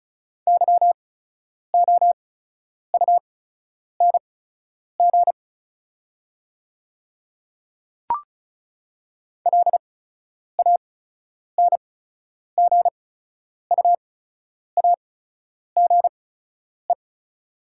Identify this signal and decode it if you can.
Morse code, heard as a keyed tone, about 34 wpm, spaced as for 10 wpm, 700 Hz. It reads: YOUNG LANGUAGE
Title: 8x Character Spacing - Top 500 Words - 35wpm